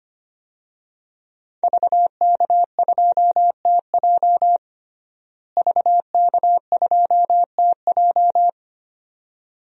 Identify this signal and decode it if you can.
Morse code, heard as a keyed tone, about 25 wpm, 700 Hz. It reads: VK2TJ VK2TJ
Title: DXCC-100 Call Signs - 5 Characters - Repeated 2x - 25wpm